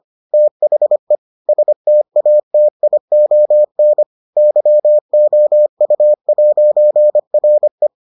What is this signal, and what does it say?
Morse code, heard as a keyed tone, about 25 wpm, 600 Hz. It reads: THE STATION YOU'RE